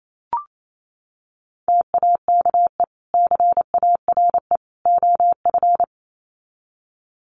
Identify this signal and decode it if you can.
Morse code, heard as a keyed tone, about 28 wpm, 700 Hz. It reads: TAKE CARE OF
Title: Sets of 3 Words 28wpm